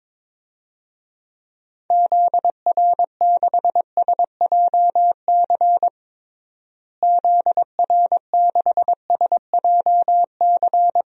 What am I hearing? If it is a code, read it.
Morse code, heard as a keyed tone, about 22 wpm, 700 Hz. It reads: ZR6SJC ZR6SJC
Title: DXCC-100 Call Signs - 6 Characters - Repeated 3x - 22wpm